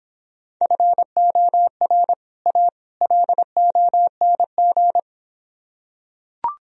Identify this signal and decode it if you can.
Morse code, heard as a keyed tone, about 26 wpm, 700 Hz. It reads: FOR A LONG